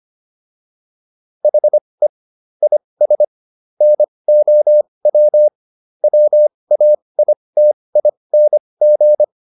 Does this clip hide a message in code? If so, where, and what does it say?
Yes: Morse code from 1.4 to 9.2 s, heard as a keyed tone, about 25 wpm, spaced as for 20 wpm, 600 Hz. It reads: HE IS NOW WAITING